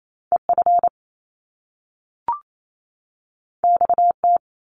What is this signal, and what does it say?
Morse code, heard as a keyed tone, about 28 wpm, 700 Hz. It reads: EF XT
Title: Top Two Letter Combinations in Words 28wpm